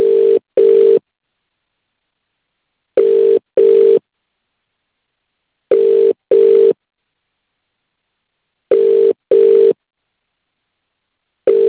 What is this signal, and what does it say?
Ringback tone (UK), 400 + 450 Hz.